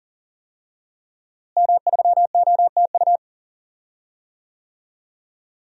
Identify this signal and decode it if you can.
Morse code, heard as a keyed tone, about 40 wpm, 700 Hz. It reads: M3OTU
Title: DXCC-100 Call Signs - 5 Characters - Repeated 3x - 40wpm